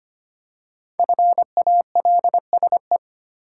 Morse code, heard as a keyed tone, about 25 wpm, 700 Hz. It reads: FALSE
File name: 5 Letter Words - First 3 Letters Given - 25wpm